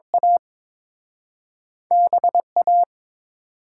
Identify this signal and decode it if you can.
Morse code, heard as a keyed tone, about 30 wpm, 700 Hz. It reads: A B A